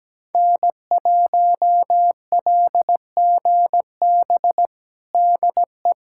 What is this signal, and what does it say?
Morse code, heard as a keyed tone, about 17 wpm, 700 Hz. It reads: N1LGB DE
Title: USA Callsign DE Callsign - 5 Characters - 17wpm